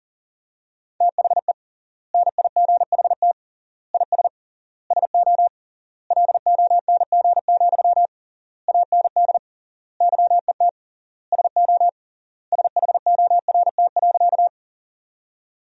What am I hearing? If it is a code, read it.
Morse code, heard as a keyed tone, about 40 wpm, 700 Hz. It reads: THE NIGHT IS SO LONG, AND YET SO SHORT.
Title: 3x Word-Spacing - Sentences from Top 500 Words - 40wpm